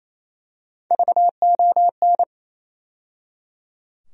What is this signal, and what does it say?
Morse code, heard as a keyed tone, about 28 wpm, 700 Hz. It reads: VON
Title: USA Call Sign Postfixes - 3 Letters 28wpm